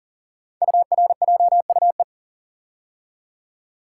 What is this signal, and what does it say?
Morse code, heard as a keyed tone, about 38 wpm, 700 Hz. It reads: URJUE
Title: Code Groups - 5 Letters - 40wpm